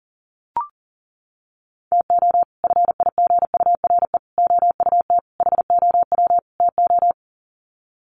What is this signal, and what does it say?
Morse code, heard as a keyed tone, about 40 wpm, 700 Hz. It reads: TO FIGURE OUT HOW TO
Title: Sets of 5 Words 40wpm